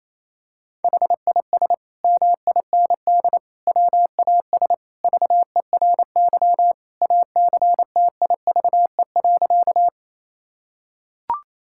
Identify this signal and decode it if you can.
Morse code, heard as a keyed tone, about 28 wpm, 700 Hz. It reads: HIS MIND WAS VERY ACTIVE.
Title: Sentences from Top 1000 Words - Review Part 6 - 28wpm